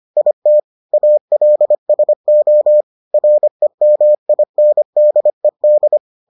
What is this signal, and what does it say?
Morse code, heard as a keyed tone, about 25 wpm, 600 Hz. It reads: IT ALSO REMINDED